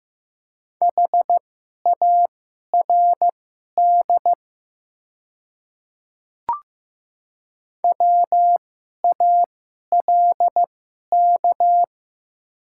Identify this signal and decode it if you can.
Morse code, heard as a keyed tone, about 15 wpm, spaced as for 11 wpm, 700 Hz. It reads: HARD WALK
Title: 2x Character Spacing - Top 500 Words - 15wpm